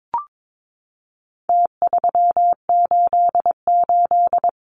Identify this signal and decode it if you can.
Morse code, heard as a keyed tone, about 22 wpm, 700 Hz. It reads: T388
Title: Numbers - Contest Style - 4-Digit 22wpm